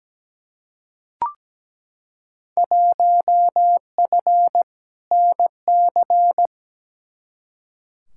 Morse code, heard as a keyed tone, about 17 wpm, 700 Hz. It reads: 1F NC
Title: Field Day Exchanges - 17wpm